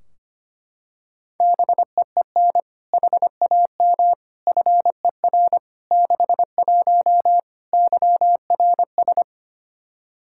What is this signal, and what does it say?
Morse code, heard as a keyed tone, about 25 wpm, 700 Hz. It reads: BEEN HAM FER 61 YRS
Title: K1USN SST QSO Practice 25wpm